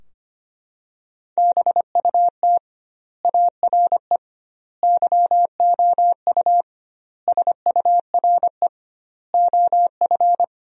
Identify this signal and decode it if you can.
Morse code, heard as a keyed tone, about 25 wpm, 700 Hz. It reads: BUT ARE YOU SURE OF
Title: 2x Word-Spacing - Sentences from Top 300 Words - 25wpm